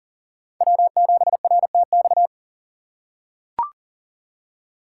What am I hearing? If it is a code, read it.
Morse code, heard as a keyed tone, about 40 wpm, 700 Hz. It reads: W7RTX
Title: USA Call Signs - 3x Character Spacing - 40wpm